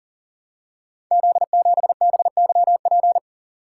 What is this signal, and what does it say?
Morse code, heard as a keyed tone, about 39 wpm, 700 Hz. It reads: Z7BYP
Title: Code Groups - 5 Letter:Numbers - 40wpm